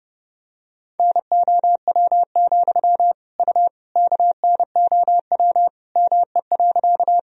Morse code, heard as a keyed tone, about 30 wpm, 700 Hz. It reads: NOW, U KNOW ME.
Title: Sentences from Top 200 Words and Abbreviations 30wpm